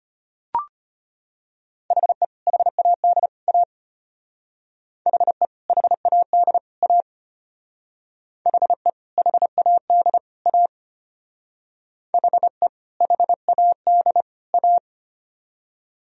Morse code, heard as a keyed tone, about 33 wpm, 700 Hz. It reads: HE HAD A HE HAD A HE HAD A HE HAD A